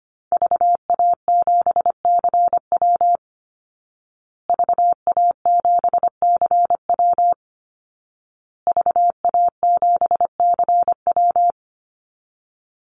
Morse code, heard as a keyed tone, about 25 wpm, 700 Hz. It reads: VA7CW VA7CW VA7CW
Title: Call Signs Repeated 2x - 25wpm